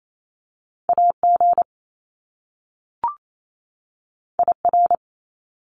Morse code, heard as a keyed tone, about 28 wpm, 700 Hz. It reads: AG IR
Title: Top Two Letter Combinations in Words 28wpm